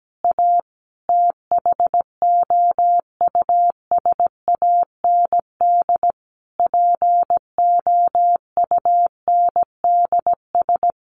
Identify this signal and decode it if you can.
Morse code, heard as a keyed tone, about 17 wpm, 700 Hz. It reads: A THOUSAND POUNDS
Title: A Sound Of Thunder 17wpm